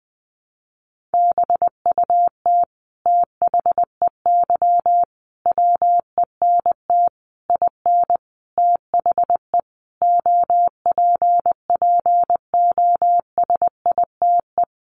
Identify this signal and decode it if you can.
Morse code, heard as a keyed tone, about 19 wpm, 700 Hz. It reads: BUT THEY WENT IN THE OPPOSITE
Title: Sentences from Top 1300 Words 20wpm